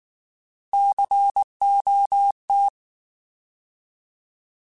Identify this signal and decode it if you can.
Morse code, heard as a keyed tone, about 19 wpm, 800 Hz. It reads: COT
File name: Alphabetical / Alphabetical mp3s / Cs @18